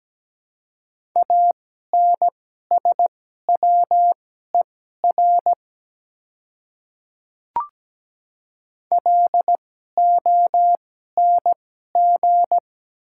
Morse code, heard as a keyed tone, about 17 wpm, spaced as for 12 wpm, 690 Hz. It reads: ANSWER LONG
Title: 2x Character Spacing - Top 300 Words - 17wpm